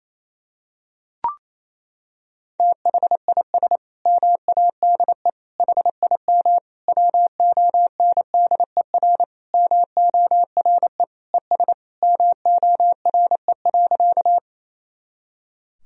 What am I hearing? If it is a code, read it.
Morse code, heard as a keyed tone, about 28 wpm, 700 Hz. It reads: THIS MADE HIM WONDER MORE ES MORE.
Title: Sentences from Top 500 Words and Abbreviations 28wpm